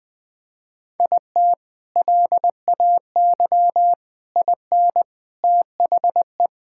Morse code, heard as a keyed tone, about 20 wpm, 700 Hz. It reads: IT LAY IN THE
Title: Sentences from Top 1000 Words - Review Part 6 - 20wpm